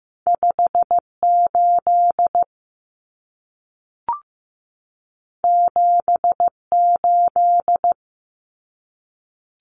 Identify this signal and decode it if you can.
Morse code, heard as a keyed tone, about 15 wpm, 700 Hz. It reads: 58 78
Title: Common Cut Numbers - 2-Digit 15wpm